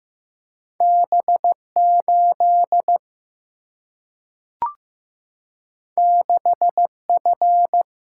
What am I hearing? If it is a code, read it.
Morse code, heard as a keyed tone, about 15 wpm, 700 Hz. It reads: B 8 6 F
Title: Single-Letter Single-Number 15wpm